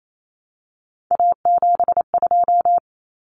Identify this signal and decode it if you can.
Morse code, heard as a keyed tone, about 28 wpm, 700 Hz. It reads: A72
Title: Common Cut Numbers - 3-Digit 28wpm